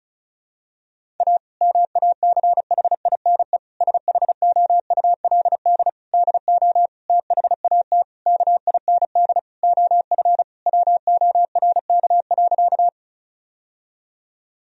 Morse code, heard as a keyed tone, about 36 wpm, 700 Hz. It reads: A MACHINE SHOULD DO THAT KIND OF WORK.
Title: Sentences from Top 1000 Words - Review Part 2 - 35wpm